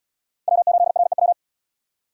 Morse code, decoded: W8REW

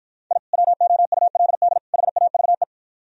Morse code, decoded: I WOULD HAVE

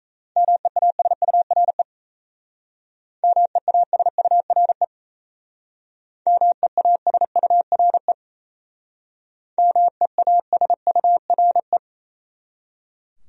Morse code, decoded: MEASURE MEASURE MEASURE MEASURE

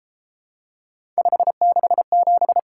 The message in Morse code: 567

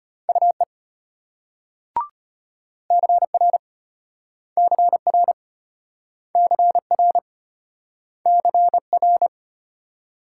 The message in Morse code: UE CR CR CR CR

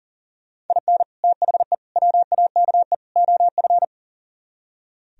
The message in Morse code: IN THE WAKE OF

